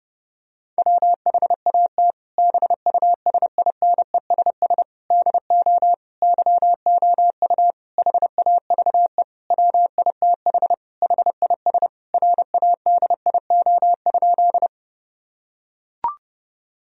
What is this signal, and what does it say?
Morse code, heard as a keyed tone, about 32 wpm, 700 Hz. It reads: WHAT BUSINESS DO YOU HAVE WITH HIS RADIO?
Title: Sentences from Top 1000 Words - Review Part 3 - 30wpm